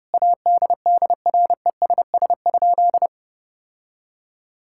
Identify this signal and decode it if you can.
Morse code, heard as a keyed tone, about 32 wpm, 700 Hz. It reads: ADDRESS?
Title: Sentences from Top 1000 Words - Review Part 6 - 30wpm